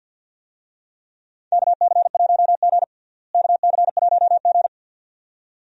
Morse code, heard as a keyed tone, about 52 wpm, 700 Hz. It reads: KC1G KC1G
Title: USA Call Signs - 4 Characters - Repeated 2x - 50wpm